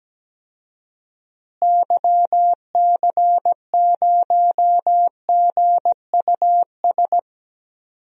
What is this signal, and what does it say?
Morse code, heard as a keyed tone, about 17 wpm, 700 Hz. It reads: YC0GUS
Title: DXCC-100 Call Signs - 6 Characters - Repeated 3x - 17wpm